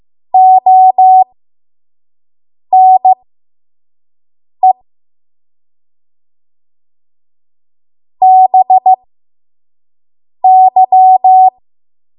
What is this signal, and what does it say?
Morse code, heard as a keyed tone, about 15 wpm, spaced as for 5 wpm, 750 Hz. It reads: ONE BY
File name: Morse / 05WPM